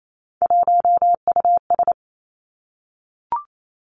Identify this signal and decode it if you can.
Morse code, heard as a keyed tone, about 28 wpm, 700 Hz. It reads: 1US
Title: Three Challenging Letter-Number Combinations 28wpm